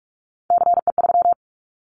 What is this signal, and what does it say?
Morse code, heard as a keyed tone, about 44 wpm, 700 Hz. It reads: CE3
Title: Callsign Prefixes for Top 100 DXCC - 45wpm